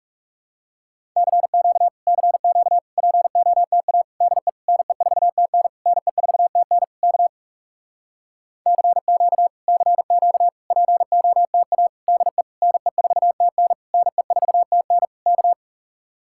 Morse code, heard as a keyed tone, about 37 wpm, 700 Hz. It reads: CQ CQ POTA DE NE4TN NE4TN K CQ CQ POTA DE NE4TN NE4TN K
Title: Speed-Racing Common QSO Phrases - 30wpm